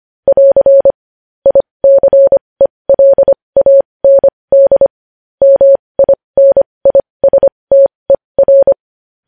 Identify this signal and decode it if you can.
Morse code, heard as a keyed tone, about 25 wpm, spaced as for 20 wpm, 550 Hz. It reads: <AR> ICELAND MINISTER
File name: podcast / News headlines-1x-25x20